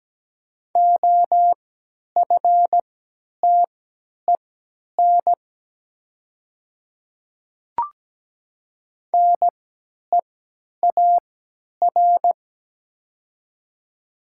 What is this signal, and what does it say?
Morse code, heard as a keyed tone, about 17 wpm, spaced as for 10 wpm, 700 Hz. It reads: OFTEN NEAR
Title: 3x Character Spacing - Top 300 Words - 17wpm